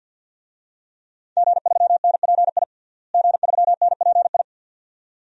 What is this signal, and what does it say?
Morse code, heard as a keyed tone, about 53 wpm, 700 Hz. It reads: G3NPI G3NPI
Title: DXCC-100 Call Signs - 5 Characters - Repeated 2x - 50wpm